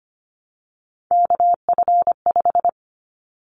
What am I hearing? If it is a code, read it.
Morse code, heard as a keyed tone, about 25 wpm, 700 Hz. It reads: KF5
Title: Callsign Prefixes for Top 25 DXCC - 25wpm